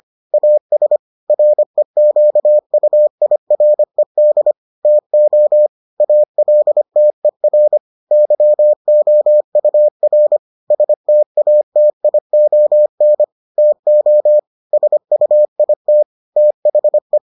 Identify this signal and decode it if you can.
Morse code, heard as a keyed tone, about 25 wpm, 600 Hz. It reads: AS REQUIRED TO ALTER YOUR STATION TO SUIT THE